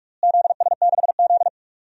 Morse code, heard as a keyed tone, about 41 wpm, 700 Hz. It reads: ZS6Z
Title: DXCC-100 Call Signs - 4 Characters - Repeated 2x - 45wpm